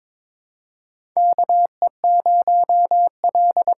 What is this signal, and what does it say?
Morse code, heard as a keyed tone, about 22 wpm, 700 Hz. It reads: KE0L
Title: USA Call Signs - 6 Characters - 8x Character Spacing - 22wpm